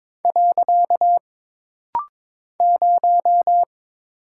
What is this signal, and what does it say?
Morse code, heard as a keyed tone, about 22 wpm, 700 Hz. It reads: . 0
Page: This rapid-fire timing is geared to push your ICR (Instant Character Recognition) skills.